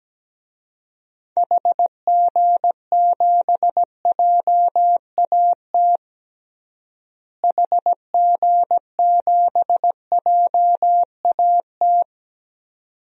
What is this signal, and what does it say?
Morse code, heard as a keyed tone, about 17 wpm, 700 Hz. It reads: HG7JAT HG7JAT
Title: DXCC-100 Call Signs - 6 Characters - Repeated 2x - 17wpm